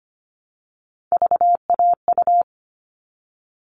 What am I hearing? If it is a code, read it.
Morse code, heard as a keyed tone, about 25 wpm, 700 Hz. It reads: VAU
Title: All Cut Numbers - 3-Digit 25wpm